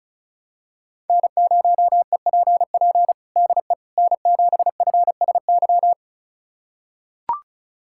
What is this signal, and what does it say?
Morse code, heard as a keyed tone, about 36 wpm, 700 Hz. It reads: N0EPP DE N7FSY
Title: USA Callsign DE Callsign - 5 Characters - 35wpm